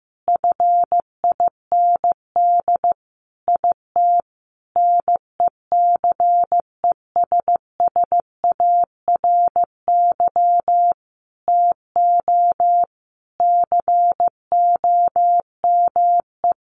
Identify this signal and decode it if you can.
Morse code, heard as a keyed tone, about 15 wpm, 700 Hz. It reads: FIND IT NECESSARY TO COME
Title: Sentences from Top 1300 Words 15wpm